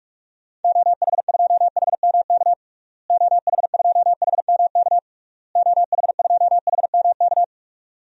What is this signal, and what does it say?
Morse code, heard as a keyed tone, about 46 wpm, 700 Hz. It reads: OH2HMK OH2HMK OH2HMK
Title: DXCC-100 Call Signs - 6 Characters - Repeated 3x - 45wpm